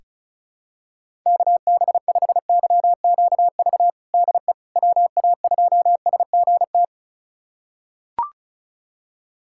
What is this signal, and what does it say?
Morse code, heard as a keyed tone, about 36 wpm, 700 Hz. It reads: KB5YQV DE WA2SGT
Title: USA Callsign DE Callsign 35wpm